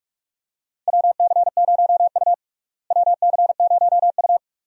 Morse code, heard as a keyed tone, about 46 wpm, 700 Hz. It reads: WC0U WC0U
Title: USA Call Signs - 4 Characters - Repeated 2x - 45wpm